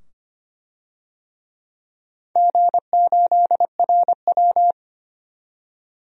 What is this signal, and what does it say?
Morse code, heard as a keyed tone, about 25 wpm, 700 Hz. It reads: G8RW
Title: DXCC-100 Call Signs - 4 Characters - 25wpm